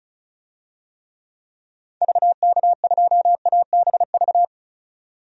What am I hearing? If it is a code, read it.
Morse code, heard as a keyed tone, about 34 wpm, 700 Hz. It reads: VK2ABV
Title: DXCC-100 Call Signs - 6 Characters - 35wpm